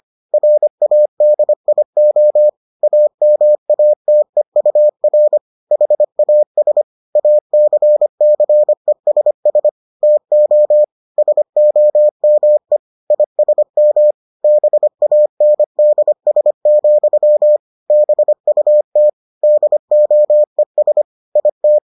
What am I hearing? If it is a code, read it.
Morse code, heard as a keyed tone, about 25 wpm, 600 Hz. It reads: RADIO AMATEUR HAS ACCESS TO SOME ISM BANDS, BUT DOES IT